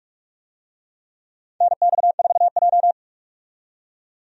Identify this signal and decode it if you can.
Morse code, heard as a keyed tone, about 45 wpm, 700 Hz. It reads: NX4J
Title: USA Call Signs - 4 Characters - 45wpm